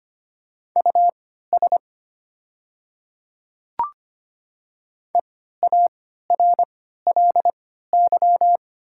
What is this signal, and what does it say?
Morse code, heard as a keyed tone, about 25 wpm, spaced as for 14 wpm, 700 Hz. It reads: US EARLY
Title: 3x Character Spacing - Top 500 Words - 25wpm